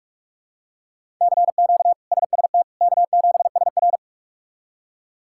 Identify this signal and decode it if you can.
Morse code, heard as a keyed tone, about 46 wpm, 700 Hz. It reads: CQ SST K7SR
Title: K1USN SST QSO Practice - 45wpm